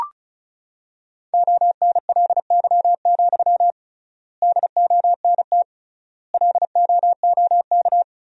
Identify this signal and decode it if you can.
Morse code, heard as a keyed tone, about 36 wpm, 700 Hz. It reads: ONLY, DONT LOOK
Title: 3x Word-Spacing - Sentences from Top 300 Words - 35wpm